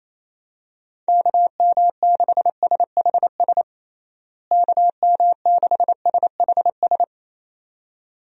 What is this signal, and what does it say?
Morse code, heard as a keyed tone, about 28 wpm, 700 Hz. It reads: KM6SHS KM6SHS
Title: USA Call Signs - 6 Characters - Repeated 2x - 28wpm